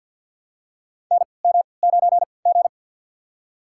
Morse code, heard as a keyed tone, about 53 wpm, spaced as for 29 wpm, 700 Hz. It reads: NM9G
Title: USA Call Signs - 4 Characters - 3x Character Spacing - 50wpm